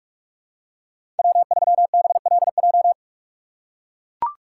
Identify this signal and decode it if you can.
Morse code, heard as a keyed tone, about 41 wpm, 700 Hz. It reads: W3BLJ
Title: USA Call Signs - 5 Characters - 3x Character Spacing - 45wpm